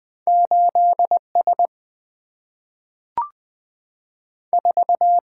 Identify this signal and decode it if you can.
Morse code, heard as a keyed tone, about 20 wpm, 700 Hz. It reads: 8 S 4
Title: Two Letter-Number Combinations - 20wpm